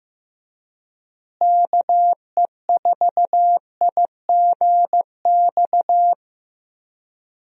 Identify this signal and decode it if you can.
Morse code, heard as a keyed tone, about 16 wpm, 700 Hz. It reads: KE4IGX
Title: USA Call Signs - 6 Characters - Repeated 3x - 15wpm